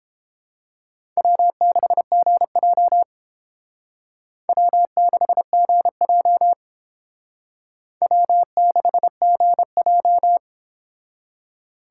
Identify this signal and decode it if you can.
Morse code, heard as a keyed tone, about 30 wpm, 700 Hz. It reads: W6GJ W6GJ W6GJ